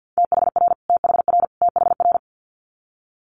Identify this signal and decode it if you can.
Morse code, heard as a keyed tone, about 50 wpm, 700 Hz. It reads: THR THR THR
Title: Multi-letter Phonemes Repeated 3x 50wpm